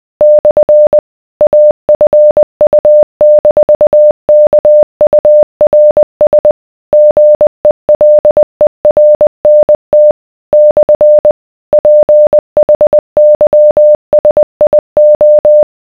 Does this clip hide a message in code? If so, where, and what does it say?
Yes: Morse code from 0.2 to 15.6 s, heard as a keyed tone, about 20 wpm, 600 Hz. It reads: / AFU-KURS GELERNT / PHYSIO